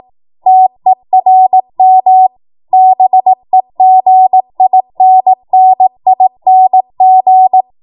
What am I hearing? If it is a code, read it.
Morse code, heard as a keyed tone, about 18 wpm, 750 Hz. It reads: TERM BEGINNING
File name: Morse / 18WPM